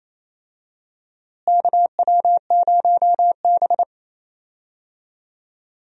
Morse code, heard as a keyed tone, about 28 wpm, 700 Hz. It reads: KW0B
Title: USA Call Signs - 4 Characters - Repeated 3x - 28wpm